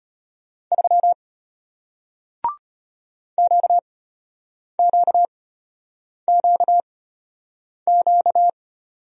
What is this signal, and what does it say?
Morse code, heard as a keyed tone, about 34 wpm, spaced as for 8 wpm, 700 Hz. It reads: 3 QQQQ